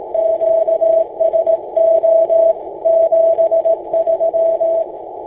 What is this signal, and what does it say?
Morse code, heard as a keyed tone, about 18 wpm, 660 Hz. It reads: QSO 73
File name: ik6llj